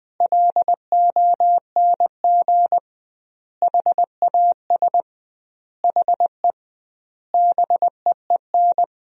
Morse code, heard as a keyed tone, about 20 wpm, 700 Hz. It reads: LONG HAS HE BEEN
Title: 2x Word-Spacing - Sentences from Top 100 Words - 20wpm